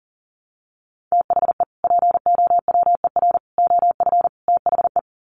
Morse code, heard as a keyed tone, about 40 wpm, 700 Hz. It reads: THE POWER OF THE
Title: Sets of 4 Words 40wpm